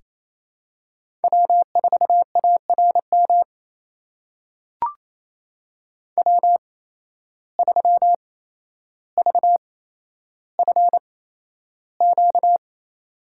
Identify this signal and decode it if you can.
Morse code, heard as a keyed tone, about 27 wpm, spaced as for 12 wpm, 700 Hz. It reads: W4ARM W3VFQ
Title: USA Call Signs - 5 Characters - 8x Character Spacing - 28wpm